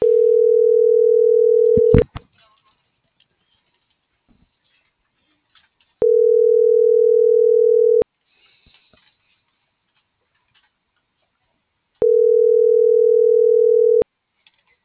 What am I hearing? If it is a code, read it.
Ringback tone (North America), 440 + 480 Hz.
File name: CENTRAL TERRESTRE